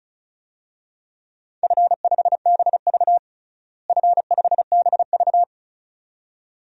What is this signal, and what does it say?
Morse code, heard as a keyed tone, about 34 wpm, 700 Hz. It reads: F5BV F5BV
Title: DXCC-100 Call Signs - 4 Characters - Repeated 2x - 35wpm